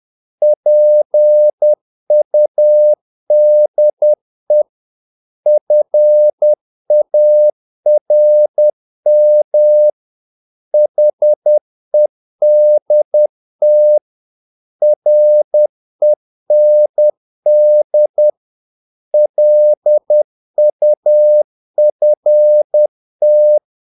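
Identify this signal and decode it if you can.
Morse code, heard as a keyed tone, about 10 wpm, 600 Hz. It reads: PUDE FARM HEDT REND LUFT
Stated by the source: Morse practice of short Danish words at 10 wpm, from CW med Gnister.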